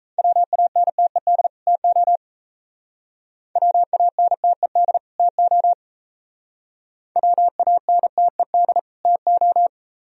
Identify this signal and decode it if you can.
Morse code, heard as a keyed tone, about 36 wpm, 700 Hz. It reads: WANTED TO WANTED TO WANTED TO